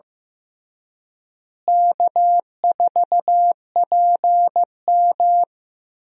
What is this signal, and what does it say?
Morse code, heard as a keyed tone, about 15 wpm, 700 Hz. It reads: K4PM
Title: USA Call Signs - 4 Characters - Repeated 3x - 15wpm